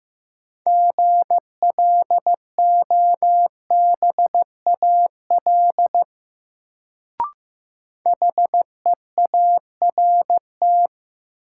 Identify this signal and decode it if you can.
Morse code, heard as a keyed tone, about 15 wpm, 700 Hz. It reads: GLOBAL HEART
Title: Sentences from Top 1200 Words - New Word Drill - 15wpm